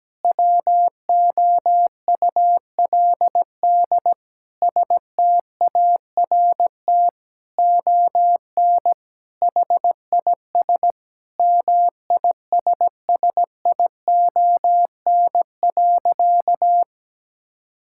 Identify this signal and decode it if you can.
Morse code, heard as a keyed tone, about 17 wpm, 700 Hz. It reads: WOULD START ON HIS MISSION.
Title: Sentences from Top 1600 Words 17wpm